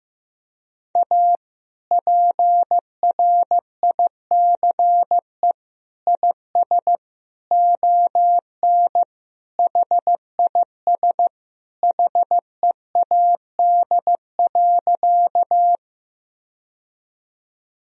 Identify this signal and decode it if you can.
Morse code, heard as a keyed tone, about 15 wpm, 700 Hz. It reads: A PRICE IS ON HIS HEAD.
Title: Sentences from Top 1000 Words - Review Part 4 - 15wpm